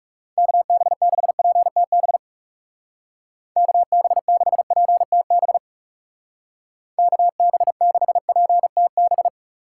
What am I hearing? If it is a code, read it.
Morse code, heard as a keyed tone, about 37 wpm, 700 Hz. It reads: KB6PTB KB6PTB KB6PTB
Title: Speed Racing - USA 2x3 Call Signs 30wpm